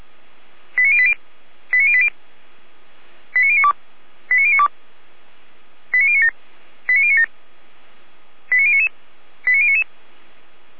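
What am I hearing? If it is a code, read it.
Five-tone selective call (ZVEI), heard as quick runs of five tones, tones of about 70 ms. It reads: 89080 89080 89092 89092 89097 89097 89099 89099
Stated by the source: ZVEI five-tone selective call received on an SDR by Feuerwehr Orlamünde.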